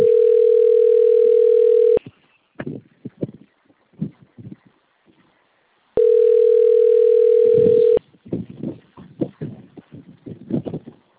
Ringback tone (North America), 440 + 480 Hz.